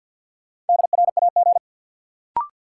Morse code, heard as a keyed tone, about 53 wpm, 700 Hz. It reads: DRAG
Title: Sentences from Top 1500 Words - New Word Drill Alt - 50wpm